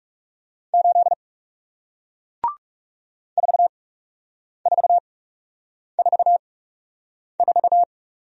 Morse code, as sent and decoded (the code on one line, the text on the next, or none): ---.. / ....- ....- ....- ....-
8 4444